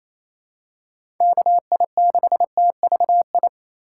Morse code, heard as a keyed tone, about 27 wpm, 700 Hz. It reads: KI6TVI
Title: USA Call Signs - 6 Characters - 4x Character Spacing - 28wpm